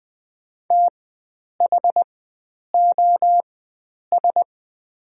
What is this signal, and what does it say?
Morse code, heard as a keyed tone, about 21 wpm, spaced as for 9 wpm, 700 Hz. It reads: THOS
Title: 4x Character Spacing - Top 500 Words - 20wpm